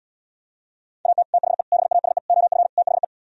Morse code, heard as a keyed tone, about 77 wpm, 700 Hz. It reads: AT THE BASE OF THE